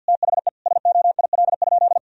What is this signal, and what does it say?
Morse code, heard as a keyed tone, about 51 wpm, 700 Hz. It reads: THE SOIL?
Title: Sentences from Top 1000 Words - Review Part 4 - 50wpm